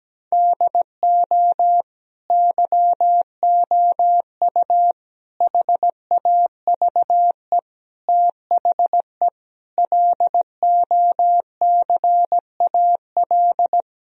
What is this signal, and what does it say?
Morse code, heard as a keyed tone, about 17 wpm, 700 Hz. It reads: DO YOU HAVE THE LOCAL
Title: Sentences from Top 1000 Words - Review Part 3 - 17wpm